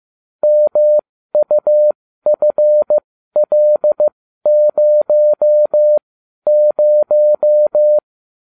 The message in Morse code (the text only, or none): MUFL00